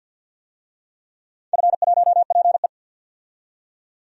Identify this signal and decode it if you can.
Morse code, heard as a keyed tone, about 51 wpm, 700 Hz. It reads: F1PE
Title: DXCC-100 Call Signs - 4 Characters - 50wpm